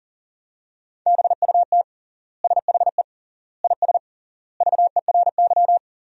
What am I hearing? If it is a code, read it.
Morse code, heard as a keyed tone, about 40 wpm, 700 Hz. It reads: BUT SHE IS VERY
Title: 3x Word-Spacing - Sentences from Top 400 Words - 40wpm